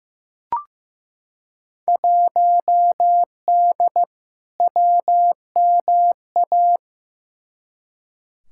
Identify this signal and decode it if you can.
Morse code, heard as a keyed tone, about 15 wpm, 700 Hz. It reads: 1D WMA
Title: Field Day Exchanges - 15wpm